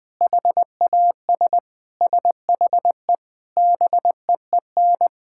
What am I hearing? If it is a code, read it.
Morse code, heard as a keyed tone, about 21 wpm, 700 Hz. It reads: HAS SHE BEEN
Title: Sentences from Top 1000 Words - Review Part 6 - 20wpm